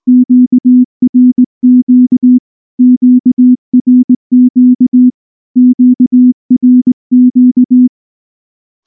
Morse code, heard as a keyed tone, about 18 wpm, 250 Hz. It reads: QRQ QRQ QRQ
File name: morse-training-20WPM